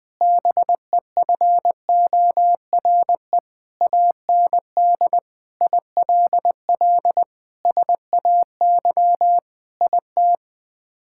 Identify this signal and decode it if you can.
Morse code, heard as a keyed tone, about 21 wpm, 700 Hz. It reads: BEFORE AND ILL SAY IT
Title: Sets of 8 Words - 20wpm